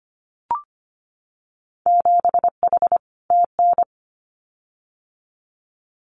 Morse code, heard as a keyed tone, about 25 wpm, 700 Hz. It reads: 7H TN
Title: Winter Field Day Exchanges Encore 25wpm